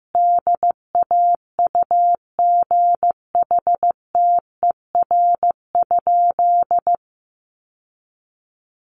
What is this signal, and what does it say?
Morse code, heard as a keyed tone, about 15 wpm, 700 Hz. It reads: DAUGHTER?
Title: Sentences from Top 1500 Words 15wpm